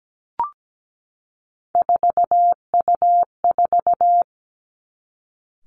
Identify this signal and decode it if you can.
Morse code, heard as a keyed tone, about 17 wpm, 700 Hz. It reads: 4U4
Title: Three Challenging Letter-Number Combinations 17wpm